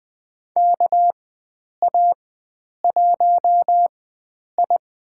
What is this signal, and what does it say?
Morse code, heard as a keyed tone, about 20 wpm, spaced as for 9 wpm, 700 Hz. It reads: KA1I
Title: USA Call Signs - 4x Character Spacing - 20wpm